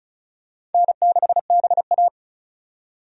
Morse code, heard as a keyed tone, about 34 wpm, 700 Hz. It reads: N6BA
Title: USA Call Signs - 4 Characters - 35wpm